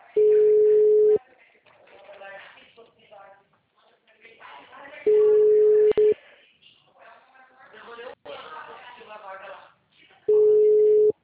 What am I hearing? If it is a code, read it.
Ringback tone (Europe, 425 Hz), with speech.
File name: out- -2013- - - .19948